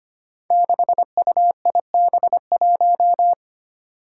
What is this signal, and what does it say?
Morse code, heard as a keyed tone, about 25 wpm, 700 Hz. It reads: 6UIB1
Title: Five Challenging Letter-Number Combinations 25wpm